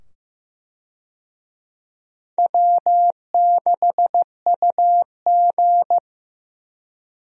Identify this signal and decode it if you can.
Morse code, heard as a keyed tone, about 15 wpm, 700 Hz. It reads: W6UG
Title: USA Call Signs - 4 Characters - 15wpm